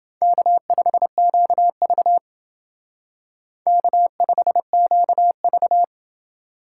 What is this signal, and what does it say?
Morse code, heard as a keyed tone, about 31 wpm, 700 Hz. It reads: K5QV K5QV